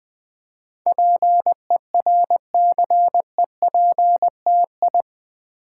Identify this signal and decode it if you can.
Morse code, heard as a keyed tone, about 20 wpm, 700 Hz. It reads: PERCEPTI
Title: Build A Word - 4 Letters At A Time - 10-Letter Words - 20wpm